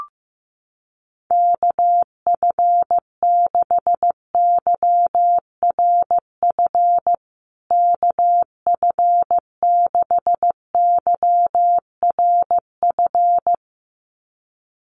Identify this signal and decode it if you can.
Morse code, heard as a keyed tone, about 15 wpm, 700 Hz. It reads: KF6YRF KF6YRF
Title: USA Call Signs Repeated 2x 15wpm